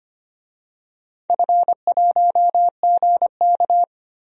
Morse code, heard as a keyed tone, about 25 wpm, 700 Hz. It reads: F1GK
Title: DXCC-100 Call Signs - 4 Characters - 25wpm